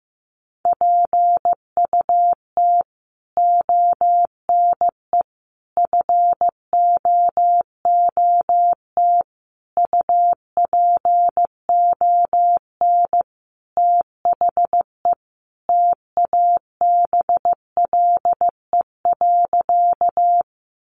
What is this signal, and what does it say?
Morse code, heard as a keyed tone, about 15 wpm, 700 Hz. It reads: PUT ONE FOOT UPON THE TABLE.
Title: Sentences from Top 1100 Words 15wpm